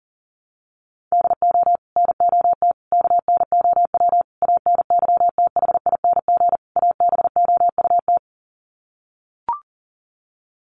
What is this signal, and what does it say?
Morse code, heard as a keyed tone, about 40 wpm, 700 Hz. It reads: DO NOT KNOW ANYTHING ABOUT
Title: Sets of 5 Words 40wpm